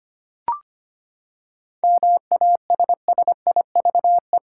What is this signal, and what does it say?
Morse code, heard as a keyed tone, about 25 wpm, 700 Hz. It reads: MASSIVE
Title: Seven Letter Words - 25wpm